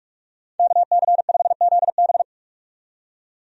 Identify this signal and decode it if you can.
Morse code, heard as a keyed tone, about 46 wpm, 700 Hz. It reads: KC5ZB
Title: USA Call Signs - 5 Characters - 45wpm